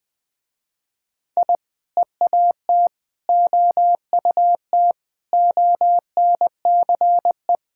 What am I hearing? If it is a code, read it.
Morse code, heard as a keyed tone, about 19 wpm, 700 Hz. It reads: I EAT OUT ONCE
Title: Sentences with 4-Word Phrases - Spoken Beginning - 20wpm